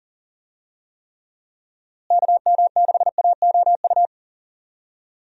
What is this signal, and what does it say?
Morse code, heard as a keyed tone, about 40 wpm, 700 Hz. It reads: KM6AOU
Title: USA Call Signs - 6 Characters - Repeated 3x - 40wpm